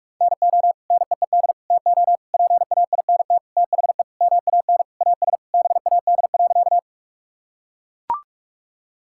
Morse code, heard as a keyed tone, about 42 wpm, 700 Hz. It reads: NO NEED TO PAINT THE MAN AS BAD.
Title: Sentences from Top 1000 Words - Review Part 6 - 45wpm